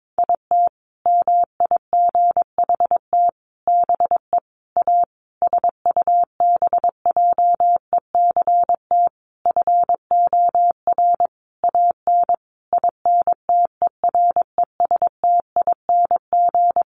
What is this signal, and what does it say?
Morse code, heard as a keyed tone, about 22 wpm, 700 Hz. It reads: IT MIGHT BE A SUBJECT FOR AN INTERESTING
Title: Sentences from Top 1500 Words 22wpm